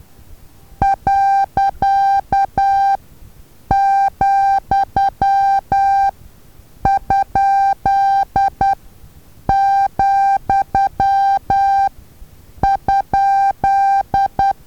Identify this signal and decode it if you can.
Morse code, heard as a keyed tone, about 10 wpm, spaced as for 7 wpm, 790 Hz. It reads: .,?,?
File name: Taviro HA6VA A